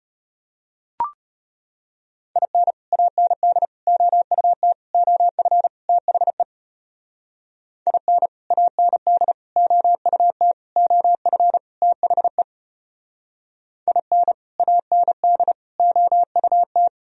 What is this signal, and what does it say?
Morse code, heard as a keyed tone, about 34 wpm, 700 Hz. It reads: IN AND OUT OF THE IN AND OUT OF THE IN AND OUT